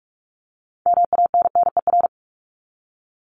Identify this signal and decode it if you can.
Morse code, heard as a keyed tone, about 44 wpm, 700 Hz. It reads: MANNER
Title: Top 1501-1750 Words 45wpm